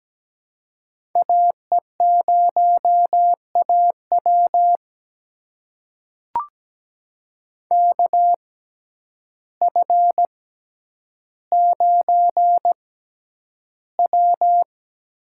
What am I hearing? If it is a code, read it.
Morse code, heard as a keyed tone, about 17 wpm, spaced as for 9 wpm, 700 Hz. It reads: AE0AW KF9W
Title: USA Call Signs - 5 Characters - 6x Character Spacing - 17wpm